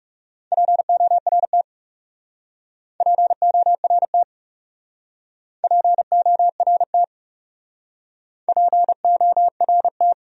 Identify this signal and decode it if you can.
Morse code, heard as a keyed tone, about 37 wpm, 700 Hz. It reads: PORT PORT PORT PORT
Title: Speed-Racing Top 300 Words 30wpm